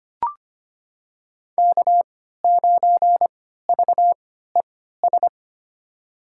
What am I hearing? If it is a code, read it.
Morse code, heard as a keyed tone, about 25 wpm, spaced as for 14 wpm, 700 Hz. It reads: K9VES
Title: USA Call Signs - 5 Characters - 3x Character Spacing - 25wpm